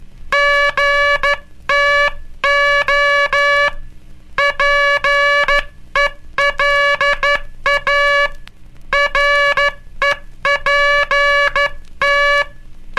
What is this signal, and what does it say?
Morse code, heard as a keyed tone, about 12 wpm, 1700 Hz, with noise. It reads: GTO PELA REPT
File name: cursos / Curso CW mp3 / CW Aula - 18